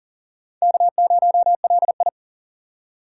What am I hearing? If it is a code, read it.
Morse code, heard as a keyed tone, about 51 wpm, 700 Hz. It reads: K0LI